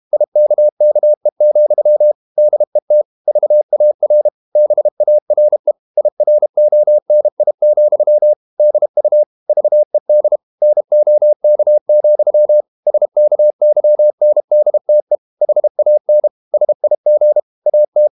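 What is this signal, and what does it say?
Morse code, heard as a keyed tone, about 32 wpm, 600 Hz. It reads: IKKE, DET VAR BARE IRONI, DU VED NOK, SKYNDTE HAN SIG AT
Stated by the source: High-speed QRQ CW at 32 WPM.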